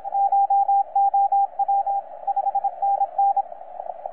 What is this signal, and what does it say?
Morse code, heard as a keyed tone, about 27 wpm, 800 Hz, with radio noise. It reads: CHOW 5NN
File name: 9M6/KM0O 30 CW